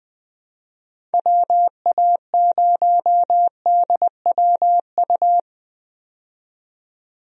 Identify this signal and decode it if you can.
Morse code, heard as a keyed tone, about 20 wpm, 700 Hz. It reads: WA0DWU
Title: USA Call Signs - 6 Characters - 20wpm